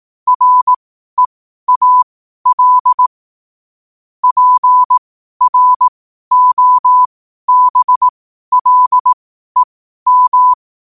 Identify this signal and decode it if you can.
Morse code, heard as a keyed tone, about 18 wpm, spaced as for 12 wpm, 1000 Hz. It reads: REAL PROBLEM
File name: qotdfiles / 12wpm